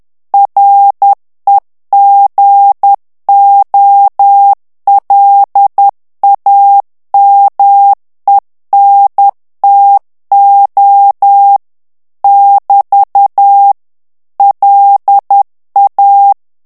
Morse code, heard as a keyed tone, about 10 wpm, 800 Hz. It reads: REGOLAMENTO <BT> LA
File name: ITALIEN texte clair 2 vitesse 10 mots minute